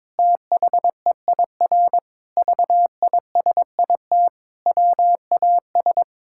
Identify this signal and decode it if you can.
Morse code, heard as a keyed tone, about 22 wpm, 700 Hz. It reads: THEIR VISIT WAS
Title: Sentences from Top 1600 Words 22wpm